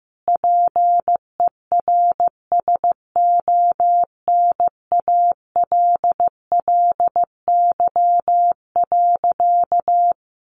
Morse code, heard as a keyed tone, about 15 wpm, 700 Hz. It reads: PERSONALLY.
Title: Sentences from Top 1400 Words 15wpm